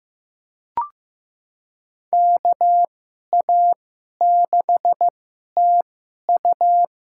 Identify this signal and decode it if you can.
Morse code, heard as a keyed tone, about 15 wpm, spaced as for 11 wpm, 700 Hz. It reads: KA6TU
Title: USA Call Signs - 2x Character Spacing - 15wpm